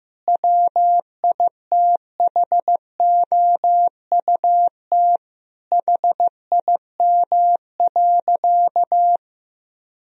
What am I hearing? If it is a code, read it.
Morse code, heard as a keyed tone, about 15 wpm, 700 Hz. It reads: WITHOUT HIM.
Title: Sentences from Top 1000 Words - Review Part 3 - 15wpm